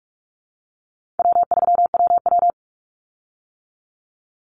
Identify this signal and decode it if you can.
Morse code, heard as a keyed tone, about 44 wpm, 700 Hz. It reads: W3WW
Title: Easy Call Signs - 45wpm